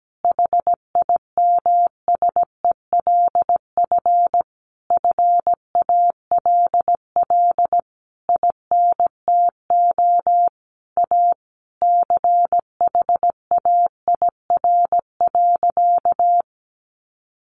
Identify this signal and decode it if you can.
Morse code, heard as a keyed tone, about 17 wpm, 700 Hz. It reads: HIMSELF FALL INTO A CHAIR.
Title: A Sound Of Thunder 17wpm